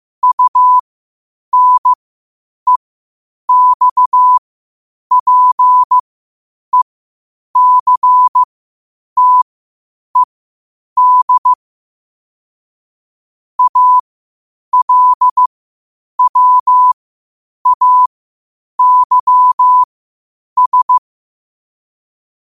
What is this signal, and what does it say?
Morse code, heard as a keyed tone, about 15 wpm, spaced as for 8 wpm, 1000 Hz. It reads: UNEXPECTED ALWAYS
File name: qotdfiles / 8wpm